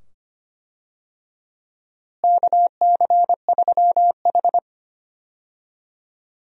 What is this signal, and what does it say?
Morse code, heard as a keyed tone, about 25 wpm, 700 Hz. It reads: KC3H
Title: USA Call Signs - 4 Characters - Repeated 2x - 25wpm